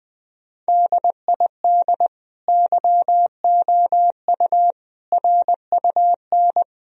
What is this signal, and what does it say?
Morse code, heard as a keyed tone, about 21 wpm, 700 Hz. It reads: DID YOU RUN
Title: Sentences from Top 1000 Words - Review Part 1 - 20wpm